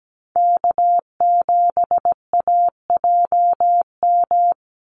Morse code, heard as a keyed tone, about 17 wpm, 700 Hz. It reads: K7AJM
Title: Call Signs - 17wpm